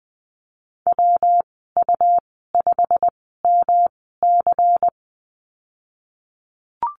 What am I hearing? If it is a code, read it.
Morse code, heard as a keyed tone, about 20 wpm, spaced as for 15 wpm, 700 Hz. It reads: WU5MC
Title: 161 Lesson 26g - Letter g - Call Signs 20wpm